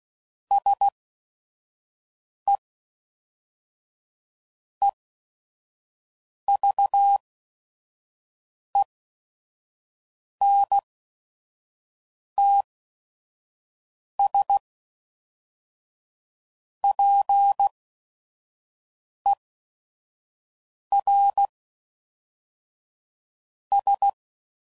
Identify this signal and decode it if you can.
Morse code, heard as a keyed tone, about 16 wpm, 800 Hz. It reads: S E E V E N T S P E R S